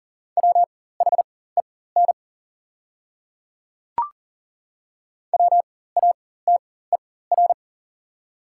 Morse code, heard as a keyed tone, about 40 wpm, spaced as for 19 wpm, 690 Hz. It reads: WHEN WATER
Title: 4x Character Spacing - Top 100 Words - 40wpm